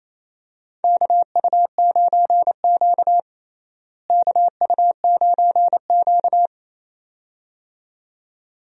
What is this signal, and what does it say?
Morse code, heard as a keyed tone, about 27 wpm, 700 Hz. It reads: KU9Q KU9Q
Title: USA Call Signs - 4 Characters - Repeated 2x - 28wpm